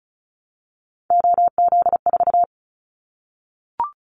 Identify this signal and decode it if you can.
Morse code, heard as a keyed tone, about 36 wpm, 700 Hz. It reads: OZ4
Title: Callsign Prefixes for Top 100 DXCC - 35wpm